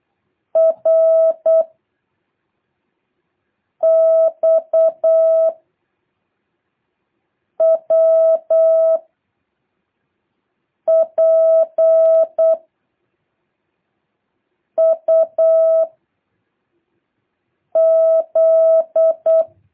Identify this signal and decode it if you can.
Morse code, heard as a keyed tone, about 8 wpm, 660 Hz. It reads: R X W P U Z
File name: jh8cbh / 2023cw